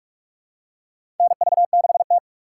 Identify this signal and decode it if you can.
Morse code, heard as a keyed tone, about 46 wpm, 690 Hz. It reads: NV6T
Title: USA Call Signs - 4 Characters - 3x Character Spacing - 45wpm